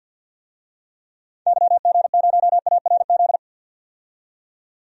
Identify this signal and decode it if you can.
Morse code, heard as a keyed tone, about 51 wpm, 700 Hz. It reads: YG0ARZ
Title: DXCC-100 Call Signs - 6 Characters - 50wpm